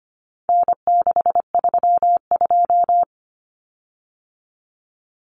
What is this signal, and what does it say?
Morse code, heard as a keyed tone, about 25 wpm, 700 Hz. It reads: N632
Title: Common Cut Numbers - 4-Digit 25wpm